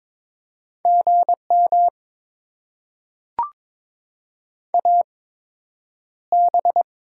Morse code, heard as a keyed tone, about 22 wpm, spaced as for 9 wpm, 700 Hz. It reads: GM AB
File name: Top 90 QSO Elements - 8x Character Spacing - 22wpm